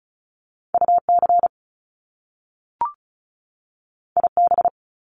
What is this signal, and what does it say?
Morse code, heard as a keyed tone, about 36 wpm, 700 Hz. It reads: UC IB
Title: Top Two Letter Combinations in Words 35wpm